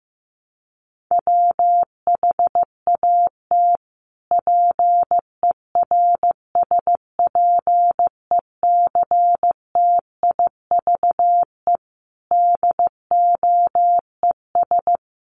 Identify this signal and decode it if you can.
Morse code, heard as a keyed tone, about 15 wpm, 700 Hz. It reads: WHAT PERSPECTIVE DOES
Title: Sentences from Top 1200 Words 15wpm